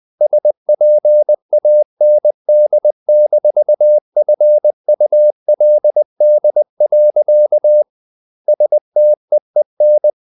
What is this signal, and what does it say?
Morse code, heard as a keyed tone, about 20 wpm, 600 Hz. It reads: SPAND-FULD. STEEN
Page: CW med Gnister